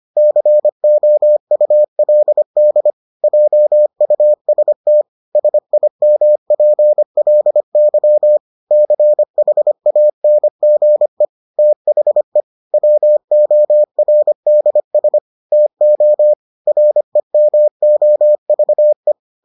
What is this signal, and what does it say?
Morse code, heard as a keyed tone, about 25 wpm, 600 Hz. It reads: COULD JUST SIMPLY CHANGE THE WORDS TO REMOVE